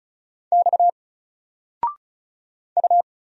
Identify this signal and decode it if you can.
Morse code, heard as a keyed tone, about 35 wpm, 700 Hz. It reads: X U
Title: Single Letter-Number Alt 35wpm